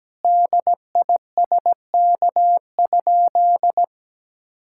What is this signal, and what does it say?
Morse code, heard as a keyed tone, about 17 wpm, 700 Hz. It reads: DISK?
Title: Sentences from Top 1000 Words - Review Part 6 - 17wpm